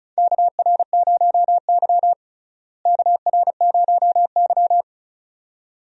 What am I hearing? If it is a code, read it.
Morse code, heard as a keyed tone, about 36 wpm, 700 Hz. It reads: KR0Y KR0Y
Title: USA Call Signs - 4 Characters - Repeated 2x - 35wpm